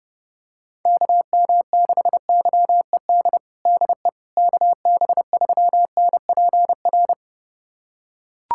Morse code, heard as a keyed tone, about 31 wpm, 700 Hz. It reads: KM6YED DE KB3NPR
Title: USA Callsign DE Callsign - 6 Characters - 30wpm